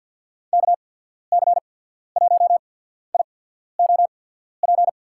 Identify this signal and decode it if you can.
Morse code, heard as a keyed tone, about 50 wpm, spaced as for 14 wpm, 700 Hz. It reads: KC1IOP
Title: USA Call Signs - 6 Characters - 8x Character Spacing - 50wpm